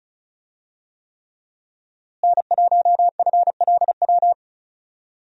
Morse code, heard as a keyed tone, about 34 wpm, 700 Hz. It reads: N1FLW
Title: USA Call Signs - 5 Characters - Repeated 3x - 35wpm